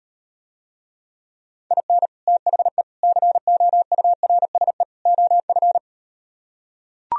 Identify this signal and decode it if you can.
Morse code, heard as a keyed tone, about 38 wpm, 700 Hz. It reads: IN THE COURSE OF